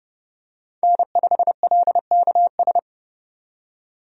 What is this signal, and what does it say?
Morse code, heard as a keyed tone, about 30 wpm, 700 Hz. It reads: N5LKS
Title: USA Call Signs - 5 Characters - 30wpm